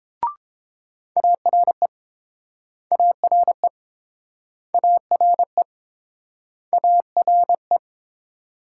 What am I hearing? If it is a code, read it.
Morse code, heard as a keyed tone, about 30 wpm, 700 Hz. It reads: ARE ARE ARE ARE